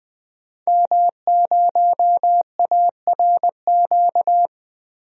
Morse code, heard as a keyed tone, about 20 wpm, 700 Hz. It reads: M0ARQ
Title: DXCC-100 Call Signs - 5 Characters - Repeated 2x - 20wpm